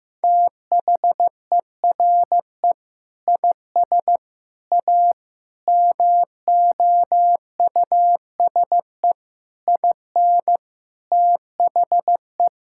Morse code, heard as a keyed tone, about 15 wpm, 700 Hz. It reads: THERE IS A MOUSE IN THE
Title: Sentences from Top 1000 Words - Review Part 5 - 15wpm